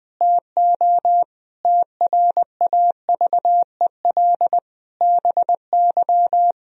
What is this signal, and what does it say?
Morse code, heard as a keyed tone, about 20 wpm, 700 Hz. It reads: TO TRAVEL BY
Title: Sentences from Top 1000 Words - Review Part 3 - 20wpm